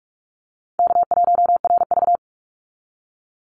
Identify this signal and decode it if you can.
Morse code, heard as a keyed tone, about 44 wpm, 700 Hz. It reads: K1RV
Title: Easy Call Signs Repeated 3x - 45wpm